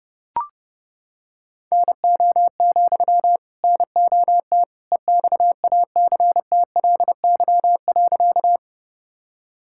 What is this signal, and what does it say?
Morse code, heard as a keyed tone, about 30 wpm, 700 Hz. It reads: NO, NOT EXACTLY.
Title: Sentences from Top 1000 Words - Review Part 6 - 30wpm